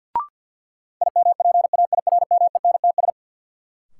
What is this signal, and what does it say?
Morse code, heard as a keyed tone, about 50 wpm, 700 Hz. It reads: IMPAIRMENTS
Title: Sets of 5 Words Encore - New Word Drill - 50wpm